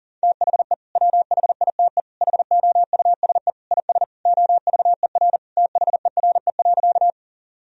Morse code, heard as a keyed tone, about 40 wpm, 700 Hz. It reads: THE WHITE HOUSE IS OVER THERE.
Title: Sentences from Top 1000 Words - Review Part 1 - 40wpm